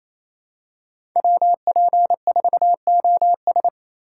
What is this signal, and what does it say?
Morse code, heard as a keyed tone, about 28 wpm, 700 Hz. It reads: WP4OS
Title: USA Call Signs - 5 Characters - 2x Character Spacing - 28wpm